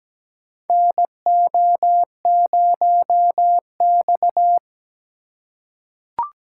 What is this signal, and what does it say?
Morse code, heard as a keyed tone, about 17 wpm, 700 Hz. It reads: NO0X
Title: USA Call Signs - 4 Characters - 2x Character Spacing - 17wpm